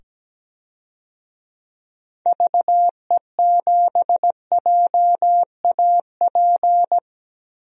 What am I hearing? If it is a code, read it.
Morse code, heard as a keyed tone, about 17 wpm, 700 Hz. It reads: VE7JAP
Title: DXCC-100 Call Signs - 6 Characters - 17wpm